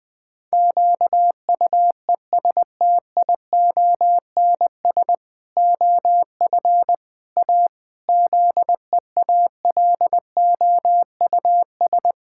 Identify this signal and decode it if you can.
Morse code, heard as a keyed tone, about 21 wpm, 700 Hz. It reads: QUESTIONS OF A ZEALOUS
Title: Pangram Challenge v2 - 20wpm